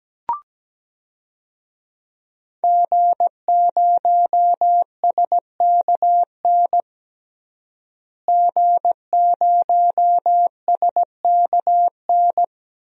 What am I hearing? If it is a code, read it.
Morse code, heard as a keyed tone, about 17 wpm, 700 Hz. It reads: G0SKN G0SKN
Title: DXCC-100 Call Signs - 5 Characters - Repeated 2x - 17wpm